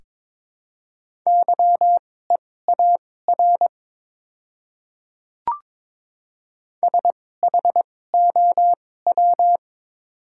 Morse code, heard as a keyed tone, about 21 wpm, spaced as for 16 wpm, 700 Hz. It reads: YEAR SHOW
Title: 2x Character Spacing - Top 200 Words - 22wpm